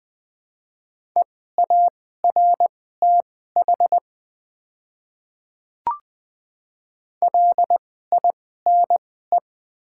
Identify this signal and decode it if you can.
Morse code, heard as a keyed tone, about 21 wpm, spaced as for 15 wpm, 700 Hz. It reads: EARTH LINE
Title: 2x Character Spacing - Top 500 Words - 20wpm